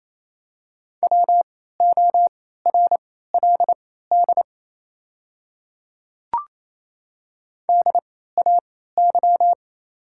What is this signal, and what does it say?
Morse code, heard as a keyed tone, about 28 wpm, spaced as for 16 wpm, 700 Hz. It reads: WORLD DAY
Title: 3x Character Spacing - Top 200 Words - 28wpm